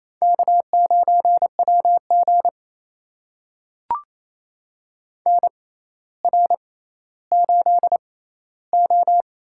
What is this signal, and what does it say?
Morse code, heard as a keyed tone, about 29 wpm, spaced as for 15 wpm, 700 Hz. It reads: K9WG NR8O
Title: USA Call Signs - 4 Characters - 6x Character Spacing - 28wpm